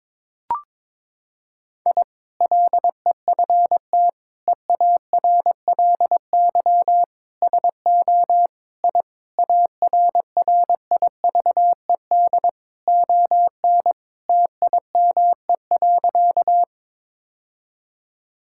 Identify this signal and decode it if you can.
Morse code, heard as a keyed tone, about 22 wpm, 700 Hz. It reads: I LEFT EARLY SO I ARRIVED ON TIME.
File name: practice / 22wpm / Sentences with 2-Word Phrases - 22wpm